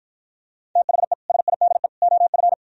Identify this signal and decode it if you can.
Morse code, heard as a keyed tone, about 52 wpm, 700 Hz. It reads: THE SIDE OF